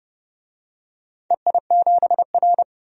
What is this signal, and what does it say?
Morse code, heard as a keyed tone, about 30 wpm, 700 Hz. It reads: EI7R
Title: DXCC-100 Call Signs - 4 Characters - 30wpm